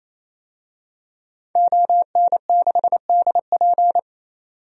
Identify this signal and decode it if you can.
Morse code, heard as a keyed tone, about 28 wpm, 700 Hz. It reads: ON6DP
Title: DXCC-100 Call Signs - 5 Characters - 28wpm